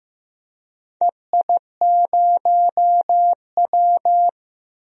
Morse code, heard as a keyed tone, about 15 wpm, 700 Hz. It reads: EI0W
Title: DXCC-100 Call Signs - 4 Characters - 15wpm